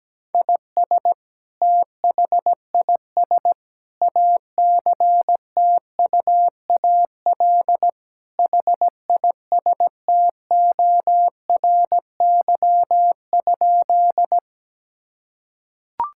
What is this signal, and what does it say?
Morse code, heard as a keyed tone, about 17 wpm, 700 Hz. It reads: IS THIS ACTUAL HISTORY?
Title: Sentences from Top 1000 Words - Review Part 5 - 17wpm